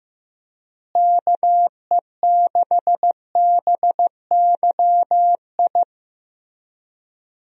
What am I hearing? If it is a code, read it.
Morse code, heard as a keyed tone, about 15 wpm, 700 Hz. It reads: KE6BYI
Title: USA Call Signs - 6 Characters - Repeated 3x - 15wpm